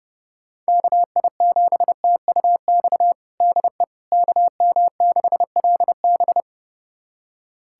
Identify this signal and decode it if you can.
Morse code, heard as a keyed tone, about 32 wpm, 700 Hz. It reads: KI7TUX DE KM6LB
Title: USA Callsign DE Callsign 30wpm